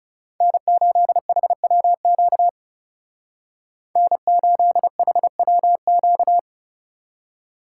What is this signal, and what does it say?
Morse code, heard as a keyed tone, about 39 wpm, 700 Hz. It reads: N8HWQ N8HWQ